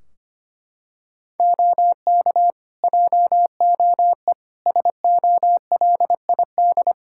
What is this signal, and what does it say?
Morse code, heard as a keyed tone, about 25 wpm, 700 Hz. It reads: OK JOE SOLID
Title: Practice QSO - Exchange 3 - More Info - 25wpm